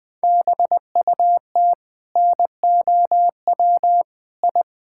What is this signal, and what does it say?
Morse code, heard as a keyed tone, about 20 wpm, 700 Hz. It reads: BUT NOW I
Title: Sentences from Top 1000 Words - Review Part 6 - 20wpm